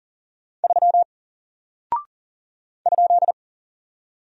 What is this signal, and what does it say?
Morse code, heard as a keyed tone, about 40 wpm, 700 Hz. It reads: 3 ?
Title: Single Character (Letter:Number:Punctuation:Prosign) Alt 40wpm